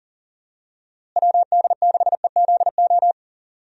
Morse code, heard as a keyed tone, about 39 wpm, 700 Hz. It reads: WD6EZO
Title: USA Call Signs - 6 Characters - Repeated 3x - 40wpm